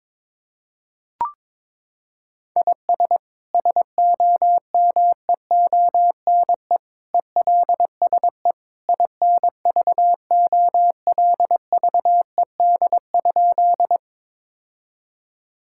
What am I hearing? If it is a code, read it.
Morse code, heard as a keyed tone, about 22 wpm, 700 Hz. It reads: IS SOMEONE ELSE INVOLVED?
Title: Sentences from Top 1000 Words - Review Part 5 - 22wpm